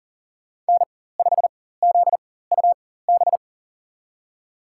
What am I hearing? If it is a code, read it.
Morse code, heard as a keyed tone, about 40 wpm, spaced as for 19 wpm, 700 Hz. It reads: N5ZUB
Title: USA Call Signs - 5 Characters - 4x Character Spacing - 40wpm